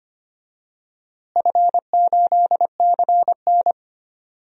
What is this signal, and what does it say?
Morse code, heard as a keyed tone, about 25 wpm, 700 Hz. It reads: F8CN
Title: DXCC-100 Call Signs - 4 Characters - Repeated 3x - 25wpm